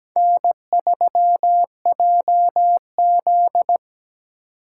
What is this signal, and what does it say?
Morse code, heard as a keyed tone, about 20 wpm, 700 Hz. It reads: N3JZ
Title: Speed Racing - USA Call Signs 17wpm